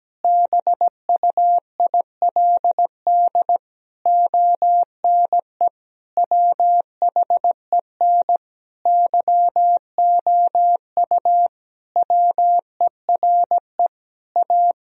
Morse code, heard as a keyed tone, about 17 wpm, 700 Hz. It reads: BUILD ONE WHEN YOU WERE A
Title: Sentences from Top 1000 Words - Review Part 1 - 17wpm